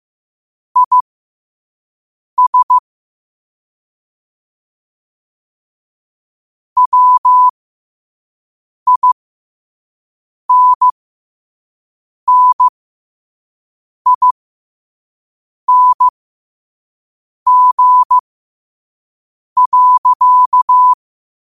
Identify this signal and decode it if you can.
Morse code, heard as a keyed tone, about 15 wpm, spaced as for 5 wpm, 1000 Hz. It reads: IS WINNING.
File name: qotdfiles / 5wpm